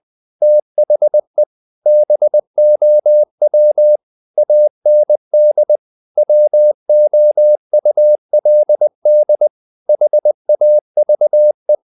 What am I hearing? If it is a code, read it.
Morse code, heard as a keyed tone, about 20 wpm, 600 Hz. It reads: THE BOW AND WOULD HAVE
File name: the old man and the sea / the old man and the sea part 1 20wpm